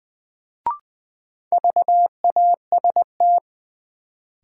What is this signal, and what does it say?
Morse code, heard as a keyed tone, about 20 wpm, 700 Hz. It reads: VAST
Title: Sets of 4 Words - New Word Drill - 20wpm